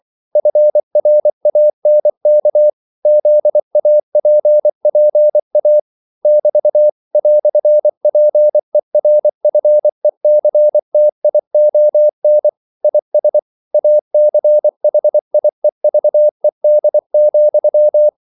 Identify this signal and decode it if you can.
Morse code, heard as a keyed tone, about 24 wpm, 600 Hz. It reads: FRANK ZAPPA <BT> "PERFECTION IS ACHIEVED,
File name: CW / Practice / EnglishQuotations0224